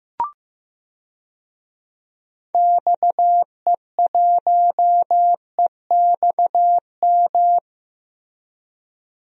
Morse code, heard as a keyed tone, about 15 wpm, 700 Hz. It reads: XE1EXM
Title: DXCC-100 Call Signs - 6 Characters - Repeated 3x - 15wpm